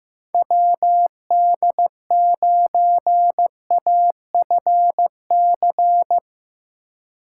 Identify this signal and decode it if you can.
Morse code, heard as a keyed tone, about 16 wpm, 700 Hz. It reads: WD9AFC
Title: Speed Racing - USA 2x3 Call Signs 15wpm